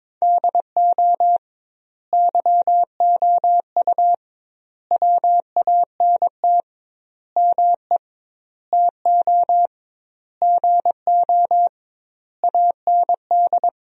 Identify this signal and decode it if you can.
Morse code, heard as a keyed tone, about 22 wpm, 700 Hz. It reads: DO YOU WANT ME TO GO AND
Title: 2x Word-Spacing - Sentences from Top 200 Words - 22wpm